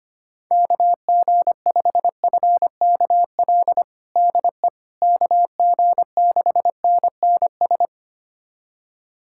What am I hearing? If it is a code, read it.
Morse code, heard as a keyed tone, about 25 wpm, 700 Hz. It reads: KG5FKL DE KG6NNS
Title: USA Callsign DE Callsign 25wpm